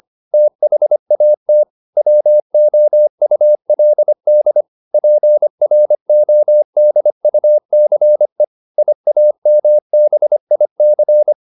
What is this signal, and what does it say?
Morse code, heard as a keyed tone, about 25 wpm, 600 Hz. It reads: THAT WOULD PRODUCE IAMBIC
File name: SS315 25